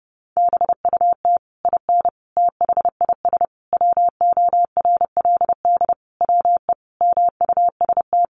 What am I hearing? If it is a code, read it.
Morse code, heard as a keyed tone, about 30 wpm, 700 Hz. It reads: BUT IN THIS WORLD WE MUST
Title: Sentences from Top 600 Words 30wpm